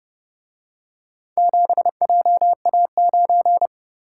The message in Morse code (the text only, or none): 7JA9